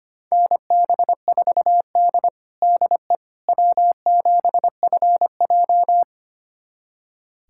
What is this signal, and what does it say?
Morse code, heard as a keyed tone, about 25 wpm, 700 Hz. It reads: NB4D DE W7FJ
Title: USA Callsign DE Callsign - 4 Characters - 25wpm